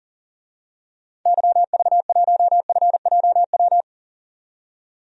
Morse code, heard as a keyed tone, about 40 wpm, 700 Hz. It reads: YV1FJW
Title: DXCC-100 Call Signs - 6 Characters - 40wpm